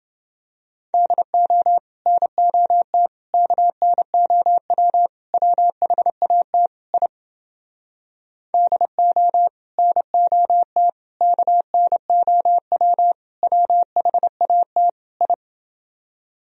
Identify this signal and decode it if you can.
Morse code, heard as a keyed tone, about 30 wpm, 700 Hz. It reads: DO NOT KNOW WHAT I DO NOT KNOW WHAT I